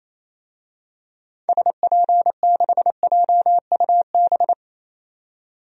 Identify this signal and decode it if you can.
Morse code, heard as a keyed tone, about 28 wpm, 700 Hz. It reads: SP6JUB
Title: DXCC-100 Call Signs - 6 Characters - 28wpm